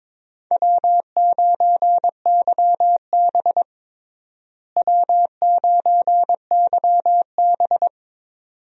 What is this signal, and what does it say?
Morse code, heard as a keyed tone, about 22 wpm, 700 Hz. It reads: W9YB W9YB
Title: USA Call Signs - 4 Characters - Repeated 2x - 22wpm